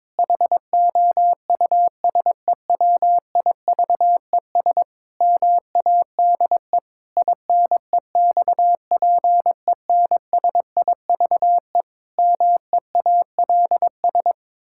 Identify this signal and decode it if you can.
Morse code, heard as a keyed tone, about 22 wpm, 700 Hz. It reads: HOUSEWIVES MADE INEXPENSIVE MEALS
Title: Pangram Challenge v2 - 22wpm